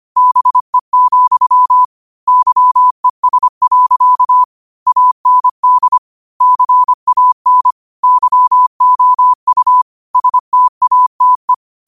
Morse code, heard as a keyed tone, about 25 wpm, 1000 Hz. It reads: BE, YES. AND CAN YOU STATE
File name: qotdfiles / 25wpm